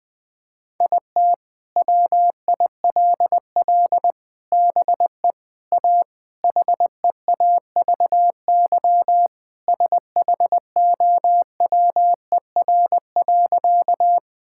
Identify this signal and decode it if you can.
Morse code, heard as a keyed tone, about 20 wpm, 700 Hz. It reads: IT WILL BE A HEAVY SHOWER.
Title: Sentences from Top 1600 Words 20wpm